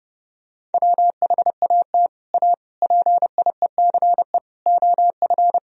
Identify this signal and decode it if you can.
Morse code, heard as a keyed tone, about 30 wpm, 700 Hz. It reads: WHAT A PIECE OF
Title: Sentences from Top 1000 Words - Review Part 2 - 30wpm